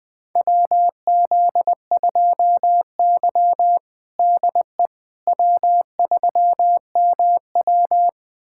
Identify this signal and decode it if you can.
Morse code, heard as a keyed tone, about 20 wpm, 700 Hz. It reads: WZ2Y DE W3MW
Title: USA Callsign DE Callsign - 4 Characters - 20wpm